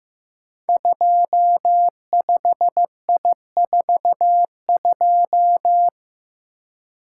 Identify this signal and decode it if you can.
Morse code, heard as a keyed tone, about 15 wpm, 700 Hz. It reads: 25I42
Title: Code Groups - 5 Letter:Numbers - 15wpm